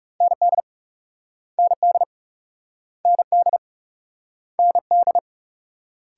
Morse code, heard as a keyed tone, about 37 wpm, 700 Hz. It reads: ND ND ND ND
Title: Speed-Racing Multi-letter Phonemes - 30wpm